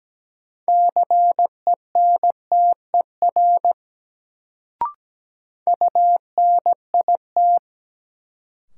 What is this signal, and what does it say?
Morse code, heard as a keyed tone, about 17 wpm, 700 Hz. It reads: CENTER UNIT
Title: Sentences from Top 1000 Words - Review Part 2 - New Word Drill - 17wpm